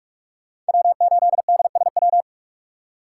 Morse code, heard as a keyed tone, about 42 wpm, 700 Hz. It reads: W8DSW
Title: USA Call Signs - 5 Characters - 6x Character Spacing - 45wpm